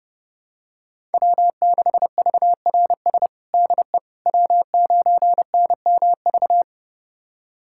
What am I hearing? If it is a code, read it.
Morse code, heard as a keyed tone, about 32 wpm, 700 Hz. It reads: W6VRS DE W9NMV
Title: USA Callsign DE Callsign - 5 Characters - 30wpm